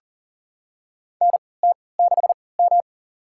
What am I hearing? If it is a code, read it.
Morse code, heard as a keyed tone, about 38 wpm, spaced as for 22 wpm, 700 Hz. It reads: NT6M
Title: USA Call Signs - 4 Characters - 3x Character Spacing - 40wpm